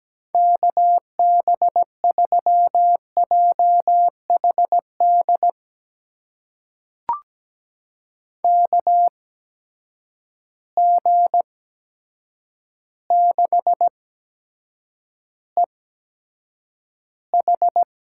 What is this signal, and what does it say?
Morse code, heard as a keyed tone, about 17 wpm, spaced as for 8 wpm, 700 Hz. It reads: KB3JHD KG6EH
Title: USA Call Signs - 6 Characters - 8x Character Spacing - 17wpm